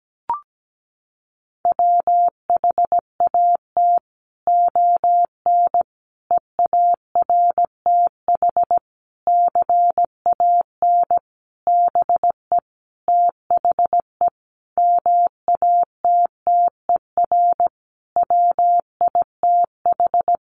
Morse code, heard as a keyed tone, about 17 wpm, 700 Hz. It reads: WHAT ON EARTH CAN BE THE MATTER WITH
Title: Sentences from Top 700 Words 17wpm